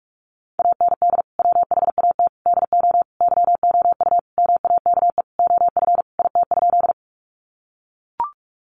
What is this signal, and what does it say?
Morse code, heard as a keyed tone, about 44 wpm, 700 Hz. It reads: AND WHAT DO YOU MAKE OF IT?
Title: Sentences from Top 100 Words 45wpm